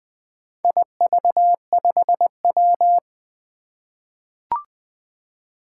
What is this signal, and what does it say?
Morse code, heard as a keyed tone, about 20 wpm, 700 Hz. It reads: IV5W
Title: Four Challenging Letter-Number Combinations 20wpm